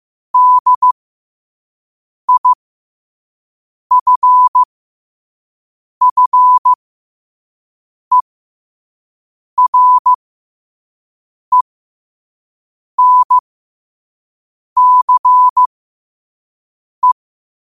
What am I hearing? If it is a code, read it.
Morse code, heard as a keyed tone, about 15 wpm, spaced as for 5 wpm, 1000 Hz. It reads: DIFFERENCE
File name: qotdfiles / 5wpm